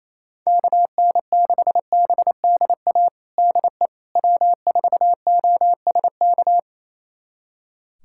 Morse code, heard as a keyed tone, about 28 wpm, 700 Hz. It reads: KN6BDA DE W4OSK
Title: USA Callsign DE Callsign 28wpm